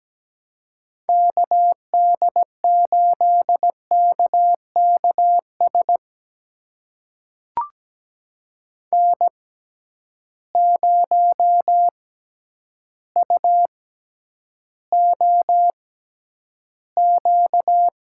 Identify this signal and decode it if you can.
Morse code, heard as a keyed tone, about 17 wpm, spaced as for 9 wpm, 700 Hz. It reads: KD8KKS N0UOQ
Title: USA Call Signs - 6x Character Spacing - 17wpm